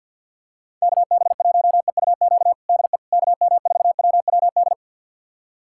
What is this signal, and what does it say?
Morse code, heard as a keyed tone, about 47 wpm, 700 Hz. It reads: KB1EUQ DE KM4WWD
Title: USA Callsign DE Callsign - 6 Characters - 50wpm